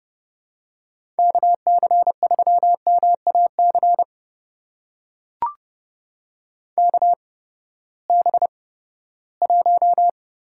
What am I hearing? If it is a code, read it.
Morse code, heard as a keyed tone, about 30 wpm, spaced as for 17 wpm, 700 Hz. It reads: KC3MAC KB1
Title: USA Call Signs - 8x Character Spacing - 30wpm